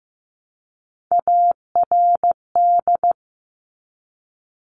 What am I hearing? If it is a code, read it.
Morse code, heard as a keyed tone, about 15 wpm, 700 Hz. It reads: ARD
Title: Top Three Letter Combinations in Words 15wpm